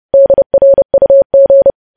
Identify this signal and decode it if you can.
Morse code, heard as a keyed tone, about 30 wpm, 550 Hz. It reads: DRUG
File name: podcast / News headlines-1x-30x30